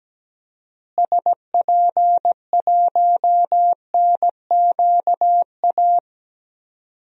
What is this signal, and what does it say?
Morse code, heard as a keyed tone, about 17 wpm, 700 Hz. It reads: SP1NQA
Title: DXCC-100 Call Signs - 6 Characters - Repeated 3x - 17wpm